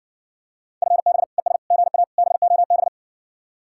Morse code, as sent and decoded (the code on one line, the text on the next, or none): ..-. -... / . ... / --. .- / -... --- -...
FB ES GA BOB